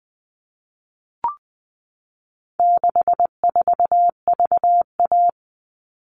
Morse code, heard as a keyed tone, about 20 wpm, 700 Hz. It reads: 64VA
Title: All Cut Numbers - 4-Digit 20wpm